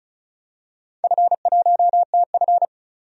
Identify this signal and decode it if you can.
Morse code, heard as a keyed tone, about 36 wpm, 700 Hz. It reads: F1TF
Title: DXCC-100 Call Signs - 4 Characters - 35wpm